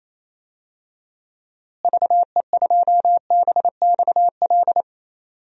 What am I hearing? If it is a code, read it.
Morse code, heard as a keyed tone, about 28 wpm, 700 Hz. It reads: VE2BXL
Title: DXCC-100 Call Signs - 6 Characters - Repeated 2x - 28wpm